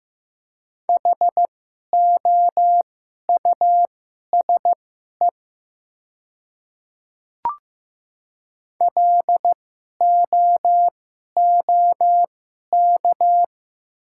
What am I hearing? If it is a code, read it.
Morse code, heard as a keyed tone, about 15 wpm, spaced as for 11 wpm, 700 Hz. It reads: HOUSE LOOK
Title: 2x Character Spacing - Top 200 Words - 15wpm